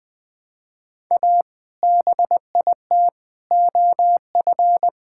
Morse code, heard as a keyed tone, about 20 wpm, 700 Hz. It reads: A BIT OF
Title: Build A Phrase - 4-Word Phrases - 20wpm